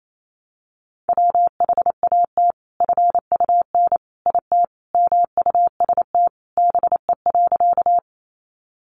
Morse code, heard as a keyed tone, about 28 wpm, 700 Hz. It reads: WHAT FUN IT MUST BE.
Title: Sentences from Top 700 Words 28wpm